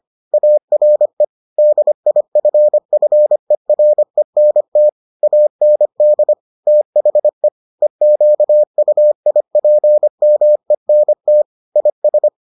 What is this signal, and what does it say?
Morse code, heard as a keyed tone, about 25 wpm, 600 Hz. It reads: ARE DIFFERENT AND THE EQUIPMENT IS